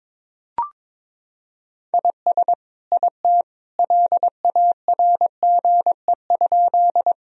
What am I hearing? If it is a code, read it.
Morse code, heard as a keyed tone, about 22 wpm, 700 Hz. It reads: IS IT LARGE?
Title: Sentences from Top 1000 Words - Review Part 1 - 22wpm